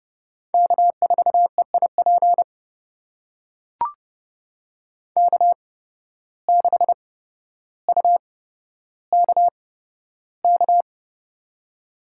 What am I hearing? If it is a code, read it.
Morse code, heard as a keyed tone, about 30 wpm, spaced as for 13 wpm, 700 Hz. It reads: K4EIP K6UKK
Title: USA Call Signs - 5 Characters - 8x Character Spacing - 30wpm